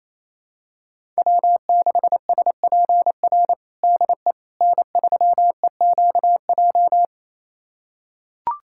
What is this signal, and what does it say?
Morse code, heard as a keyed tone, about 28 wpm, 700 Hz. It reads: W6SPR DE N3EQJ
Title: USA Callsign DE Callsign - 5 Characters - 28wpm